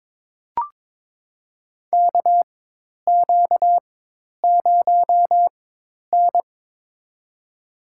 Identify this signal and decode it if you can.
Morse code, heard as a keyed tone, about 22 wpm, spaced as for 10 wpm, 700 Hz. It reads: KQ0N
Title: USA Call Signs - 4x Character Spacing - 22wpm